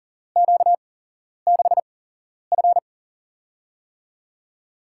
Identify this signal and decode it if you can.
Morse code, heard as a keyed tone, about 41 wpm, spaced as for 11 wpm, 690 Hz. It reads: Q6F
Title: USA Call Signs - 4 Characters - 8x Character Spacing - 40wpm